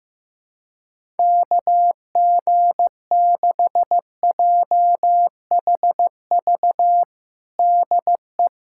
Morse code, heard as a keyed tone, about 16 wpm, 700 Hz. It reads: KG6JHV DE
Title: USA Callsign DE Callsign - 6 Characters - 15wpm